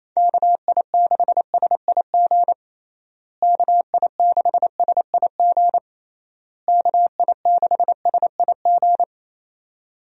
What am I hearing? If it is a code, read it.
Morse code, heard as a keyed tone, about 27 wpm, 700 Hz. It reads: KI6SIG KI6SIG KI6SIG
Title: USA Call Signs - 6 Characters - Repeated 3x - 28wpm